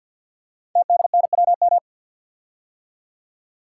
Morse code, heard as a keyed tone, about 52 wpm, 700 Hz. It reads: TDNWM
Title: Code Groups - 5 Letters - 50wpm